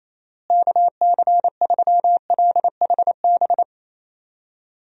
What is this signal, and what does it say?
Morse code, heard as a keyed tone, about 27 wpm, 700 Hz. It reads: KC3LHB
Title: USA Call Signs - 6 Characters - Repeated 3x - 28wpm